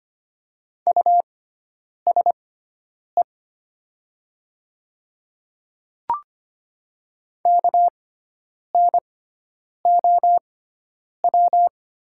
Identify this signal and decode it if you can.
Morse code, heard as a keyed tone, about 25 wpm, spaced as for 9 wpm, 700 Hz. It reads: USE KNOW
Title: 6x Character Spacing - Top 100 Words - 25wpm